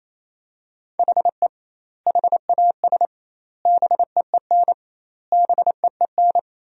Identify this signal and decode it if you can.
Morse code, heard as a keyed tone, about 28 wpm, 700 Hz. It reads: HE HAS BEEN BEEN
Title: 2x Word-Spacing - Sentences from Top 300 Words - 28wpm